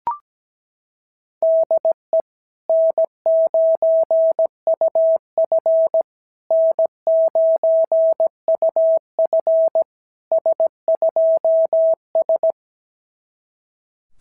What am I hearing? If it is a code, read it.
Morse code, heard as a keyed tone, about 17 wpm, 660 Hz. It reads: DE N9UF N9UF S2S
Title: SOTA QSO Practice 17wpm